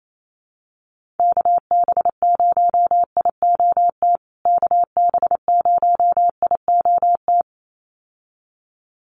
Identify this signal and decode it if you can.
Morse code, heard as a keyed tone, about 28 wpm, 700 Hz. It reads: KB0IOT KB0IOT
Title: USA Call Signs Repeated 2x 28wpm